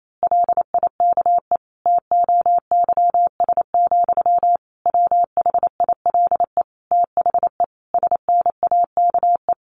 Morse code, heard as a keyed tone, about 28 wpm, 700 Hz. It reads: LIKE TOYS, WHILE THE SNAKE
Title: A Sound Of Thunder 28wpm